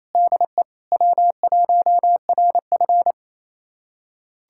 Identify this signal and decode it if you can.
Morse code, heard as a keyed tone, about 28 wpm, 700 Hz. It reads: DE W1RF
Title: USA Callsign DE Callsign - 4 Characters - 28wpm